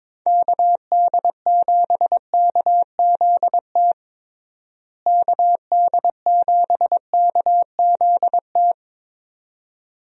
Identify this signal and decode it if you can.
Morse code, heard as a keyed tone, about 21 wpm, 700 Hz. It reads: KD7KZT KD7KZT
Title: USA Call Signs - 6 Characters - Repeated 2x - 22wpm